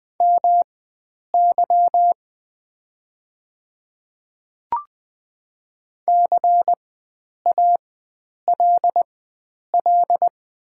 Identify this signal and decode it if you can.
Morse code, heard as a keyed tone, about 20 wpm, spaced as for 9 wpm, 690 Hz. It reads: MY CALL
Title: 4x Character Spacing - Top 100 Words - 20wpm